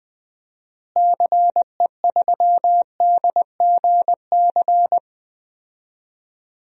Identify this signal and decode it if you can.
Morse code, heard as a keyed tone, about 20 wpm, 700 Hz. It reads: CE3DGC
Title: DXCC-100 Call Signs - 6 Characters - Repeated 2x - 20wpm